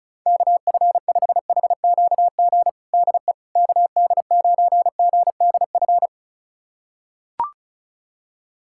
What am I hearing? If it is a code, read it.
Morse code, heard as a keyed tone, about 36 wpm, 690 Hz. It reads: KF5HQG DE KD9GDF
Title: USA Callsign DE Callsign 35wpm